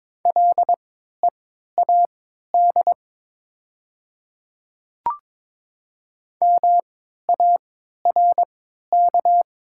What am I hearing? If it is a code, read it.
Morse code, heard as a keyed tone, about 22 wpm, spaced as for 13 wpm, 700 Hz. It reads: LEAD MARK
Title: 3x Character Spacing - Top 500 Words - 22wpm